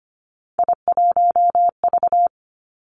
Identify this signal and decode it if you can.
Morse code, heard as a keyed tone, about 25 wpm, 700 Hz. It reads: I1V
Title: Three Challenging Letter-Number Combinations 25wpm